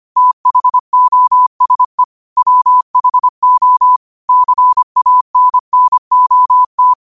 Morse code, heard as a keyed tone, about 25 wpm, 1000 Hz. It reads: THOSE WHO CANNOT
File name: qotdfiles / 25wpm